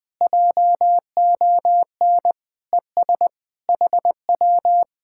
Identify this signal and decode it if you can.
Morse code, heard as a keyed tone, about 20 wpm, 700 Hz. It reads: JON ES HW